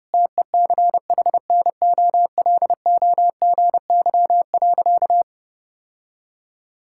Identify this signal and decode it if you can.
Morse code, heard as a keyed tone, about 30 wpm, 700 Hz. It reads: TECHNOLOGY.
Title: Sentences from Top 1000 Words - Review Part 5 - 30wpm